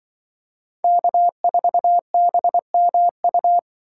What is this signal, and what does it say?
Morse code, heard as a keyed tone, about 24 wpm, 700 Hz. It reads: K4BMU
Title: Speed Racing - USA 1x3 Call Signs 20wpm